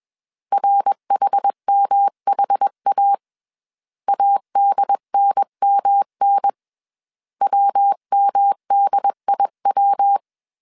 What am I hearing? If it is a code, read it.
Morse code, heard as a keyed tone, about 21 wpm, 800 Hz. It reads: RHMHA ADNMN WMDIW
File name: CW / DET / 1 3 1500C S B 20wpm 1754